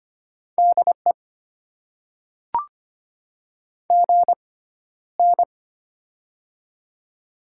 Morse code, heard as a keyed tone, about 25 wpm, spaced as for 13 wpm, 690 Hz. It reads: DE GN
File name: Top 90 QSO Elements - 6x Character Spacing - 25wpm